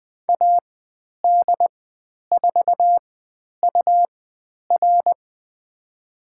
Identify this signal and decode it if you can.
Morse code, heard as a keyed tone, about 20 wpm, spaced as for 10 wpm, 700 Hz. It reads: AD4UR
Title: 109 Lesson 18g - Letter m - Call Signs 20wpm